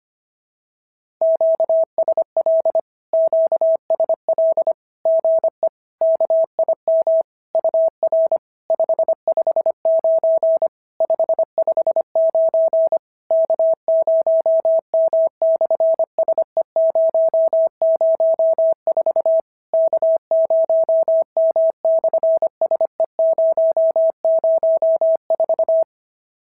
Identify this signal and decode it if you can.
Morse code, heard as a keyed tone, about 25 wpm, 650 Hz. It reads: QSL QSL GE KIM UR 559 559 K0M/SE004 K0M/SE004
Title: SOTA QSO Practice 25wpm